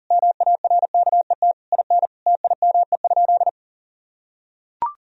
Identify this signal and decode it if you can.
Morse code, heard as a keyed tone, about 40 wpm, 700 Hz. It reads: MARKET IN TIME?
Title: Sentences from Top 1000 Words - Review Part 3 - 40wpm